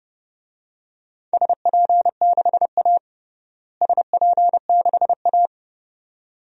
Morse code, heard as a keyed tone, about 32 wpm, 700 Hz. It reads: SP6A SP6A
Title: DXCC-100 Call Signs - 4 Characters - Repeated 2x - 30wpm